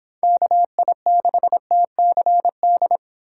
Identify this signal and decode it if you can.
Morse code, heard as a keyed tone, about 31 wpm, 700 Hz. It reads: KI6TCD